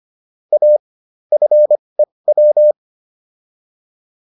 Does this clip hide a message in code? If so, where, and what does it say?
Yes: Morse code from 0.5 to 2.7 s, heard as a keyed tone, about 25 wpm, spaced as for 20 wpm, 600 Hz. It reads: A FEW